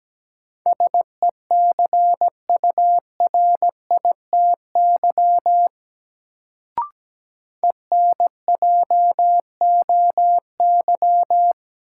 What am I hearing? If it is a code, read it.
Morse code, heard as a keyed tone, about 17 wpm, 700 Hz. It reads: SECURITY ENJOY
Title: Sentences from Top 1000 Words - Review Part 4 - New Word Drill - 17wpm